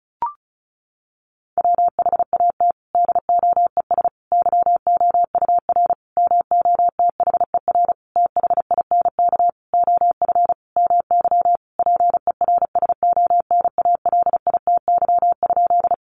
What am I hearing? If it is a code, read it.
Morse code, heard as a keyed tone, about 34 wpm, 700 Hz. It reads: WHAT DOES YOUR MOTHER THINK OF MY PERSONALITY?
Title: Sentences from Top 1400 Words 35wpm